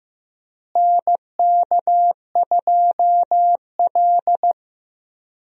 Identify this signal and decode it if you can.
Morse code, heard as a keyed tone, about 15 wpm, 700 Hz. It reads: NK2L
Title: USA Call Signs - 4 Characters - Repeated 3x - 15wpm